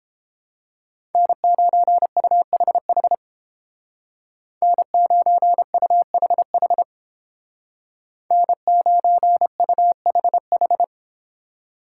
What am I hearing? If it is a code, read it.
Morse code, heard as a keyed tone, about 29 wpm, 700 Hz. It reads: N9UHH N9UHH N9UHH